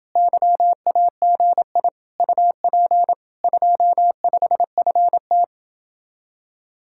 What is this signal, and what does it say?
Morse code, heard as a keyed tone, about 30 wpm, 700 Hz. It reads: YAGI UP 25FT